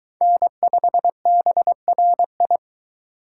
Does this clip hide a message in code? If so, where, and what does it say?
Yes: Morse code from 0.2 to 2.6 s, heard as a keyed tone, about 23 wpm, 700 Hz. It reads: N5BRI